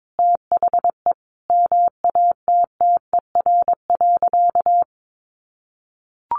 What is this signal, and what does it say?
Morse code, heard as a keyed tone, about 22 wpm, 700 Hz. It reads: THE MATTER.
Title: Sentences from Top 1200 Words 22wpm